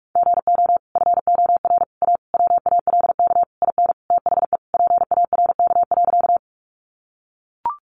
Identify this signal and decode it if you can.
Morse code, heard as a keyed tone, about 44 wpm, 700 Hz. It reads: GO FOR A WALK IN THE PARK.
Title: Sentences from Top 1300 Words 45wpm